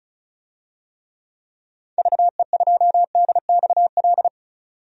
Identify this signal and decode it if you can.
Morse code, heard as a keyed tone, about 36 wpm, 690 Hz. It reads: VE2DXL
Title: DXCC-100 Call Signs - 6 Characters - 35wpm